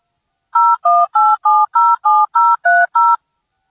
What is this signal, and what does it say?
DTMF, heard as telephone touch tones, tones of about 220 ms. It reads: 018707030